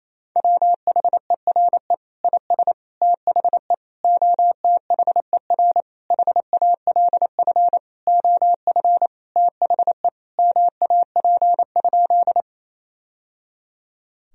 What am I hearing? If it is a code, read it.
Morse code, heard as a keyed tone, about 28 wpm, 700 Hz. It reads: WHERE IS THE OTHER HALF OF THE MAP?
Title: Sentences from Top 1000 Words - Review Part 2 - 28wpm